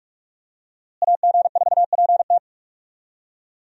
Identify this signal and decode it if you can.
Morse code, heard as a keyed tone, about 46 wpm, 700 Hz. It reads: AG4PT
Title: USA Call Signs - 5 Characters - Repeated 2x - 45wpm